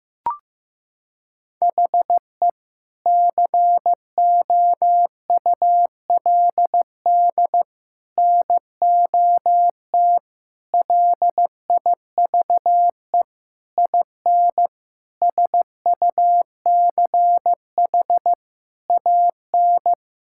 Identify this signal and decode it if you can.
Morse code, heard as a keyed tone, about 15 wpm, 700 Hz. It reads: HE COULD NOT LIVE IN SUCH AN
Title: Sentences from Top 1600 Words 15wpm